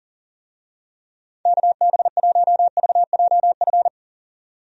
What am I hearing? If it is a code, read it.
Morse code, heard as a keyed tone, about 40 wpm, 700 Hz. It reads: KB1VJF
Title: USA Call Signs - 6 Characters - 40wpm